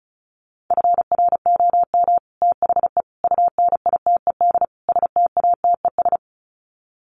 Morse code, heard as a keyed tone, about 34 wpm, 700 Hz. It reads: FROM THE UNITED STATES
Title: Sets of 4 Words 35wpm